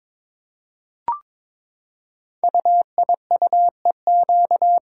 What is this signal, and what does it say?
Morse code, heard as a keyed tone, about 23 wpm, 700 Hz. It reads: UIUEQ
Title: Code Groups - 5 Letters - 22wpm